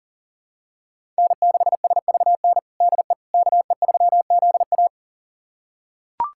Morse code, heard as a keyed tone, about 37 wpm, 700 Hz. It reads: N6SVN DE KE3ZA
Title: USA Callsign DE Callsign - 5 Characters - 40wpm